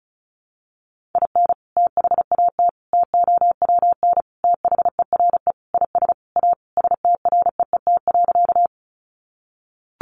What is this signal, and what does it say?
Morse code, heard as a keyed tone, about 34 wpm, 700 Hz. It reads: IN THAT TOWN THERE IS A STREET.
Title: Sentences from Top 500 Words 35wpm